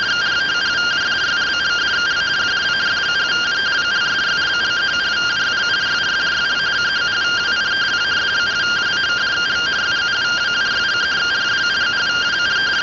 Two-tone teleprinter signal, about 170 Hz shift.